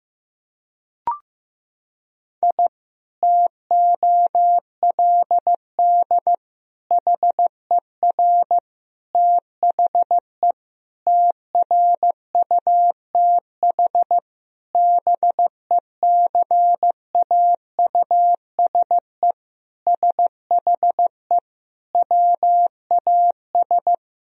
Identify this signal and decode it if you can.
Morse code, heard as a keyed tone, about 15 wpm, 700 Hz. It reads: I TOLD HER THE TRUTH BECAUSE SHE WAS
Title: Sentences from Top 1000 Words - Review Part 3 - 15wpm